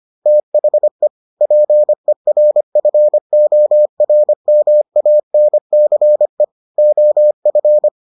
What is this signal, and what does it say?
Morse code, heard as a keyed tone, about 25 wpm, 600 Hz. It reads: THE PERFORMANCE OF